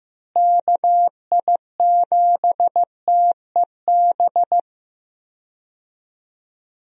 Morse code, heard as a keyed tone, about 15 wpm, 700 Hz. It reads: KI7TEB
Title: USA Call Signs - 6 Characters - Repeated 3x - 15wpm